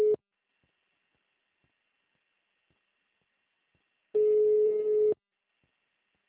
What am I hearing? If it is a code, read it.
Ringback tone (Europe, 425 Hz).